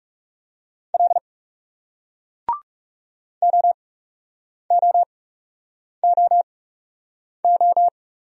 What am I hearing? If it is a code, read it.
Morse code, heard as a keyed tone, about 38 wpm, spaced as for 8 wpm, 700 Hz. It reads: L OOOO